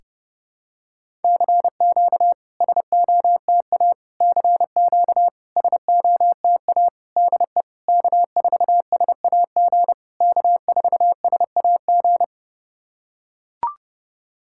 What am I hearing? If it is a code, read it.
Morse code, heard as a keyed tone, about 28 wpm, 690 Hz. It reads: CQ SOTA CQ SOTA DE K4SAG K4SAG
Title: SOTA QSO Practice 30wpm